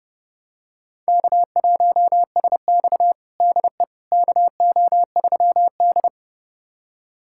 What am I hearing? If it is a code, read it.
Morse code, heard as a keyed tone, about 32 wpm, 700 Hz. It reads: K1SX DE KO3D
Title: USA Callsign DE Callsign - 4 Characters - 30wpm